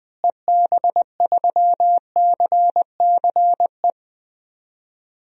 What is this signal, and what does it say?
Morse code, heard as a keyed tone, about 21 wpm, 700 Hz. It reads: EB3CCE
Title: DXCC-100 Call Signs - 6 Characters - Repeated 3x - 20wpm